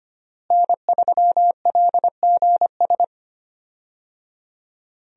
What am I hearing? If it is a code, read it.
Morse code, heard as a keyed tone, about 25 wpm, 700 Hz. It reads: N3LGS
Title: USA Call Signs - 5 Characters - Repeated 3x - 25wpm